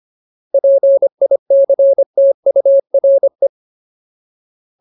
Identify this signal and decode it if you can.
Morse code, heard as a keyed tone, about 25 wpm, 550 Hz. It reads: PICTURE
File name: Ultimate Word List / 25WPM Ultimate